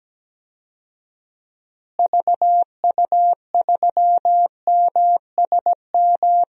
Morse code, heard as a keyed tone, about 17 wpm, 700 Hz. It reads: VU3MSM
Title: DXCC-100 Call Signs - 6 Characters - Repeated 3x - 17wpm